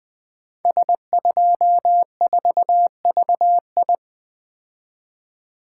Morse code, heard as a keyed tone, about 20 wpm, 700 Hz. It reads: S24VI
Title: Code Groups - 5 Letter:Numbers - 20wpm